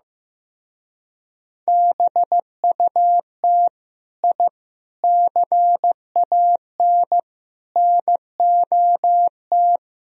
Morse code, heard as a keyed tone, about 17 wpm, 700 Hz. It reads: BUT I CAN NOT
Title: Speed-Racing - Sets of 4 Words 15wpm